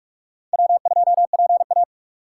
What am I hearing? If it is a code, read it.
Morse code, heard as a keyed tone, about 44 wpm, 700 Hz. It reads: W2PA
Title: USA Call Signs - 4 Characters - 2x Character Spacing - 45wpm